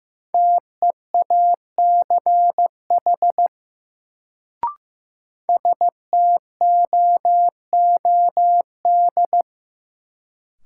Morse code, heard as a keyed tone, about 15 wpm, 700 Hz. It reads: TEACH STOOD
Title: Sentences from Top 500 Words - New Word Drill - 15wpm